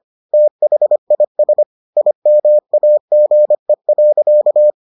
Morse code, heard as a keyed tone, about 25 wpm, 600 Hz. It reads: THIS IMAGE.